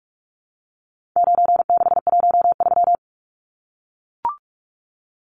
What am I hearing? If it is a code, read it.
Morse code, heard as a keyed tone, about 44 wpm, 700 Hz. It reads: 9613
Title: Numbers - 4-Digit 45wpm